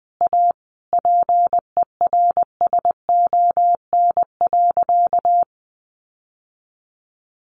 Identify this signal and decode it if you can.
Morse code, heard as a keyed tone, about 20 wpm, 700 Hz. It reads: A PERSON.
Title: Sentences from Top 1500 Words 20wpm